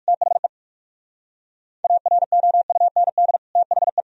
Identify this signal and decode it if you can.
Morse code, heard as a keyed tone, about 46 wpm, 690 Hz. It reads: THE AROUND THE
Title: Speed-Racing - Sets of 2 Words 45wpm